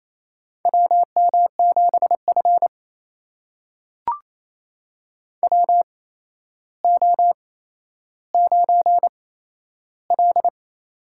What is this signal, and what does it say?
Morse code, heard as a keyed tone, about 28 wpm, spaced as for 12 wpm, 700 Hz. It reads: WM7F WO9L
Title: USA Call Signs - 4 Characters - 8x Character Spacing - 28wpm